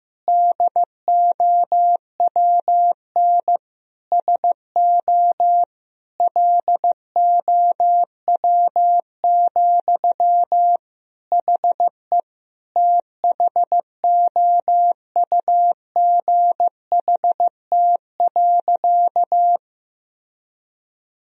Morse code, heard as a keyed tone, about 15 wpm, 700 Hz. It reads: DOWN SO LOW, HE THOUGHT.
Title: Sentences from Top 1000 Words - Review Part 1 - 15wpm